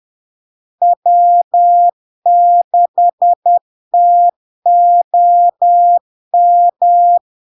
Morse code, heard as a keyed tone, about 10 wpm, 700 Hz. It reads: W6TOM
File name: LARK-FD-2022-Log-in-CW-10wpm-700-Hz